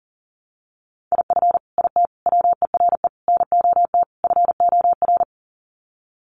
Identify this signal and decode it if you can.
Morse code, heard as a keyed tone, about 40 wpm, 700 Hz. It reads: IF IT WERE NOT FOR
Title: Sets of 5 Words 40wpm